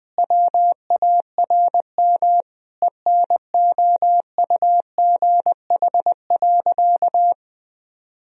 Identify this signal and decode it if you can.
Morse code, heard as a keyed tone, about 20 wpm, 700 Hz. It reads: WARM ENOUGH.
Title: Sentences from Top 1000 Words - Review Part 2 - 20wpm